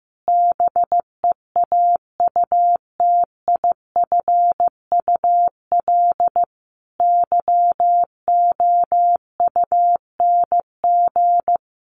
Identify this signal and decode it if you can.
Morse code, heard as a keyed tone, about 15 wpm, 700 Hz. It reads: BEAUTIFUL YOUNG
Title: Sentences from Top 1100 Words 15wpm